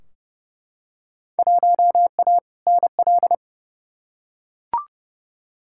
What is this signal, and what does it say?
Morse code, heard as a keyed tone, about 32 wpm, 700 Hz. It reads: 1A NL
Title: Field Day Exchanges Encore - 30wpm